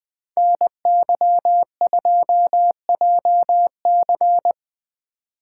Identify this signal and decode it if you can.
Morse code, heard as a keyed tone, about 23 wpm, 700 Hz. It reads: NY2JC
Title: Speed Racing - USA Call Signs 20wpm